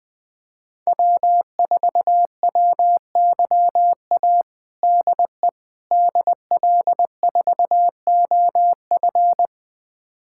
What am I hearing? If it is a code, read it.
Morse code, heard as a keyed tone, about 20 wpm, 700 Hz. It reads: W4WYA DE DL4OF
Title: USA Callsign DE Callsign - 5 Characters - 20wpm